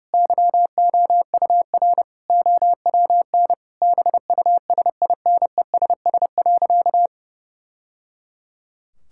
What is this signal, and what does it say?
Morse code, heard as a keyed tone, about 30 wpm, 700 Hz. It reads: YOUR OWN BUSINESS.
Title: Sentences from Top 1000 Words - Review Part 5 - 30wpm